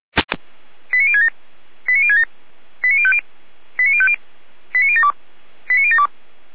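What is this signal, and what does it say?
Five-tone selective call (ZVEI), heard as quick runs of five tones, tones of about 70 ms. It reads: 89067 89067 89050 89050 89072 89072
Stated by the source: Five-tone ZVEI selective call received by SDR.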